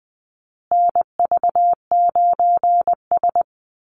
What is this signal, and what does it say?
Morse code, heard as a keyed tone, about 20 wpm, 700 Hz. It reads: NV9S
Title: USA Call Signs Repeated 3x 20wpm